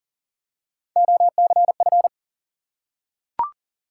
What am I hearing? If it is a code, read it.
Morse code, heard as a keyed tone, about 40 wpm, 700 Hz. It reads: OCF
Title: Top 180 QSO Elements 40wpm